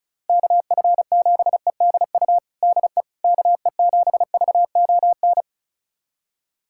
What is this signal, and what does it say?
Morse code, heard as a keyed tone, about 34 wpm, 700 Hz. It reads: KF7EDU DE KE7VON
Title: USA Callsign DE Callsign - 6 Characters - 35wpm